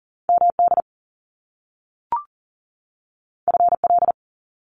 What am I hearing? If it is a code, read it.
Morse code, heard as a keyed tone, about 40 wpm, 700 Hz. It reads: MD FL
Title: US States 40wpm